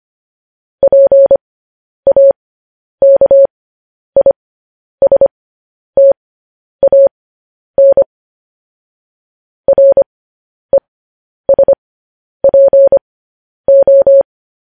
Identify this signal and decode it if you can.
Morse code, heard as a keyed tone, about 25 wpm, spaced as for 10 wpm, 550 Hz. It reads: PAKISTAN RESPO
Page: news-headlines-in-morse-code-at-10-wpm